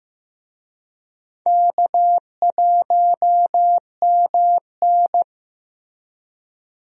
Morse code, heard as a keyed tone, about 16 wpm, 700 Hz. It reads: K1MN
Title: USA Call Signs - 4 Characters - Repeated 3x - 15wpm